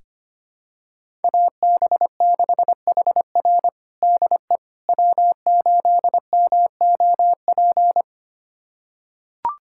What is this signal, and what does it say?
Morse code, heard as a keyed tone, about 25 wpm, 700 Hz. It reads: AB6HR DE W8MOP
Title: USA Callsign DE Callsign - 5 Characters - 25wpm